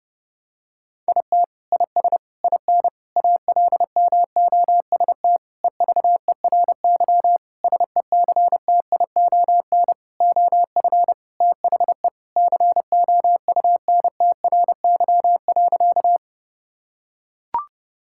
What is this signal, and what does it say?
Morse code, heard as a keyed tone, about 32 wpm, 700 Hz. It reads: IT IS IN ALMOST EVERY SECTION OF THE COUNTRY.
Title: Sentences from Top 1000 Words - Review Part 5 - 30wpm